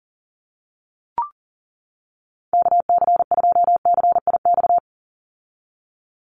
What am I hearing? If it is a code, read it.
Morse code, heard as a keyed tone, about 40 wpm, 700 Hz. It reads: KC2CIX
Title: USA Call Signs 40wpm